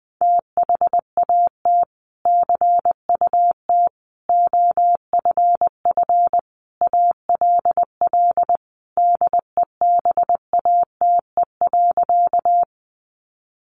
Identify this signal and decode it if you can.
Morse code, heard as a keyed tone, about 20 wpm, 700 Hz. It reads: THAT CUT OFF ALL DEBATE.
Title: Sentences from Top 1100 Words 20wpm